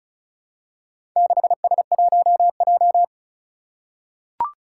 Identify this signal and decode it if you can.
Morse code, heard as a keyed tone, about 36 wpm, 700 Hz. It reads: 6S1J
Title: Four Challenging Letter-Number Combinations 35wpm